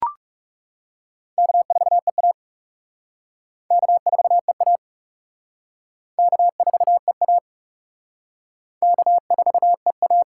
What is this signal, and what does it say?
Morse code, heard as a keyed tone, about 39 wpm, 700 Hz. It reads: K4EA K4EA K4EA K4EA